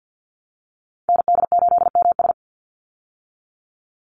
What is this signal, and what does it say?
Morse code, heard as a keyed tone, about 50 wpm, 700 Hz. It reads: ND8MS
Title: Call Signs Repeated 3x - 50wpm